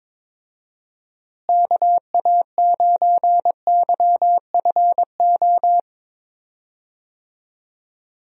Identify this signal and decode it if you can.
Morse code, heard as a keyed tone, about 22 wpm, 700 Hz. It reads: KA9YFO
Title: USA Call Signs - 6 Characters - Repeated 2x - 22wpm